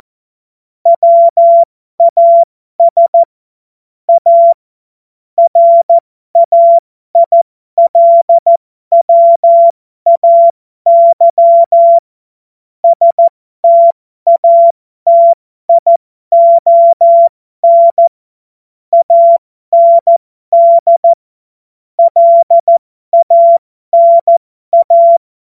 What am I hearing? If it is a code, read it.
Morse code, heard as a keyed tone, about 14 wpm, spaced as for 12 wpm, 680 Hz. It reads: WAS A RAILWAY STATION AND LANA
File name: Morsekurs / Witze